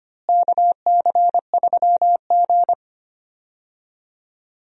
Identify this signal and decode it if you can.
Morse code, heard as a keyed tone, about 25 wpm, 700 Hz. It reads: KC3G
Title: USA Call Signs - 4 Characters - Repeated 3x - 25wpm